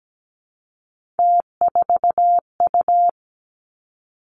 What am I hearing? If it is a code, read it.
Morse code, heard as a keyed tone, about 17 wpm, 700 Hz. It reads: T4U
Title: Three Challenging Letter-Number Combinations 17wpm